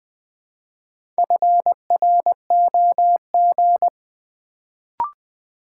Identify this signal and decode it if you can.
Morse code, heard as a keyed tone, about 20 wpm, 700 Hz. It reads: FROG
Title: Build A Word - 1 Letter At A Time - 3 to 5-Letter Words - 20wpm